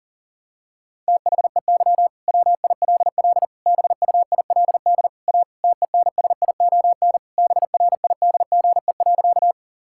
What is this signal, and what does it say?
Morse code, heard as a keyed tone, about 40 wpm, 700 Hz. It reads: THEY WILL BUILD A TENSION BRIDGE.
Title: Sentences from Top 1600 Words 40wpm